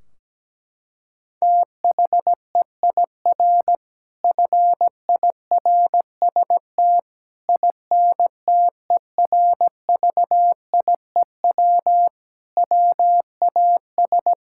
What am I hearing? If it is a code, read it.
Morse code, heard as a keyed tone, about 17 wpm, 700 Hz. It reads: THEIR FIRST INTERVIEW WAS
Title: Sentences from Top 1600 Words 17wpm